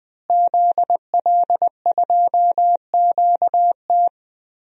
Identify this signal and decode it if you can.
Morse code, heard as a keyed tone, about 20 wpm, 700 Hz. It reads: ZL2QT
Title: DXCC-100 Call Signs - 5 Characters - Repeated 3x - 20wpm